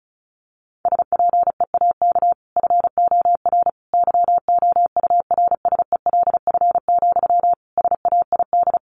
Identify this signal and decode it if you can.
Morse code, heard as a keyed tone, about 34 wpm, 700 Hz. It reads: SPEAK FOR YOURSELF, SAID
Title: Sentences from Top 1100 Words 35wpm